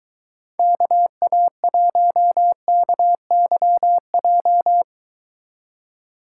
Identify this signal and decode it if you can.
Morse code, heard as a keyed tone, about 22 wpm, 700 Hz. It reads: KA1KYJ